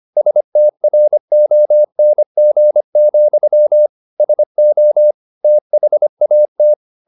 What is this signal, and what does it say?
Morse code, heard as a keyed tone, about 25 wpm, 600 Hz. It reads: STRONG, SO THAT
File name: the old man and the sea part 1 25wpm